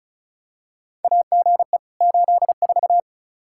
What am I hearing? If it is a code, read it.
Morse code, heard as a keyed tone, about 34 wpm, 700 Hz. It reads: AGE 84
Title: Practice QSO - Exchange 3 - More Info - 35wpm